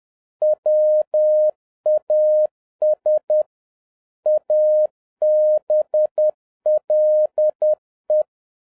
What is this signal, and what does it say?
Morse code, heard as a keyed tone, about 10 wpm, 610 Hz. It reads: WAS ABLE